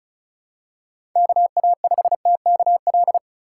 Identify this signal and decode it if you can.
Morse code, heard as a keyed tone, about 36 wpm, 700 Hz. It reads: KA5TKL
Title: USA Call Signs - 4x Character Spacing - 35wpm